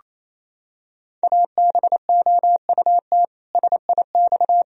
Morse code, heard as a keyed tone, about 27 wpm, 700 Hz. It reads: ABOUT SIX
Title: Sentences from Top 1000 Words - Review Part 2 - 28wpm